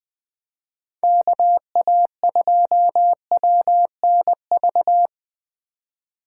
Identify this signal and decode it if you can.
Morse code, heard as a keyed tone, about 21 wpm, 700 Hz. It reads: KA2WNV
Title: USA Call Signs - 6 Characters - 3x Character Spacing - 20wpm